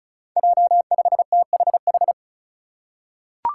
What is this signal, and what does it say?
Morse code, heard as a keyed tone, about 34 wpm, 700 Hz. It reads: J5THH
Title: Five Challenging Letter-Number Combinations 35wpm